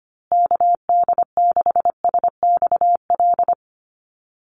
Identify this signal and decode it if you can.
Morse code, heard as a keyed tone, about 25 wpm, 700 Hz. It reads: KD6SXL
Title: USA Call Signs Repeated 2x 25wpm